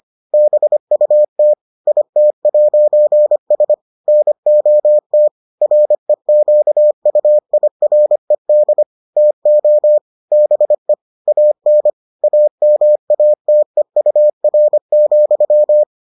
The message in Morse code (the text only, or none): BUT IT'S NOT REQUIRED TO BE AN AMATEUR,